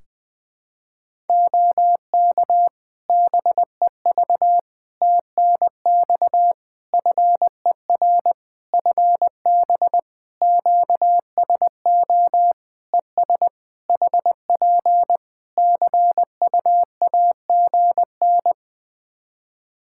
Morse code, heard as a keyed tone, about 21 wpm, 700 Hz. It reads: OK BEV TNX FER FB QSO ES HP CUAGN
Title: Practice QSO - Exchange 4 - Ending - 20wpm